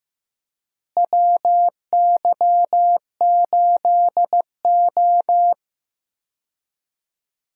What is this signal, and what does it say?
Morse code, heard as a keyed tone, about 15 wpm, 700 Hz. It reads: WY8O
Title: USA Call Signs - 4 Characters - Repeated 3x - 15wpm